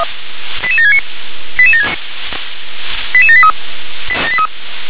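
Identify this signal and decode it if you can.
Five-tone selective call (ZVEI), heard as quick runs of five tones, tones of about 70 ms. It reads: 88769 88783
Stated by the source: Five-tone ZVEI selective call received on an SDR.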